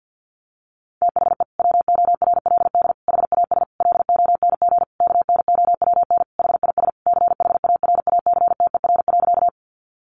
Morse code, heard as a keyed tone, about 50 wpm, 700 Hz. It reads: THE WORLD HAS LONG KNOWN HIS CHARACTER.
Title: Sentences from Top 1200 Words 50wpm